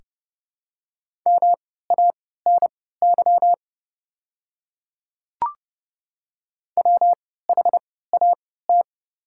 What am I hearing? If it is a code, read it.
Morse code, heard as a keyed tone, about 30 wpm, spaced as for 17 wpm, 700 Hz. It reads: MANY WHAT
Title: 3x Character Spacing - Top 100 Words - 30wpm